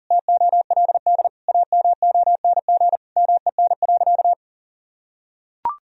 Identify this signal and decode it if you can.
Morse code, heard as a keyed tone, about 40 wpm, 700 Hz. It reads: TOLD AMONG MEN.
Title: Sentences from Top 1000 Words - Review Part 3 - 40wpm